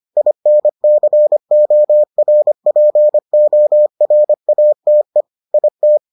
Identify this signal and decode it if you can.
Morse code, heard as a keyed tone, about 25 wpm, 600 Hz. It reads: INCORPORATE IT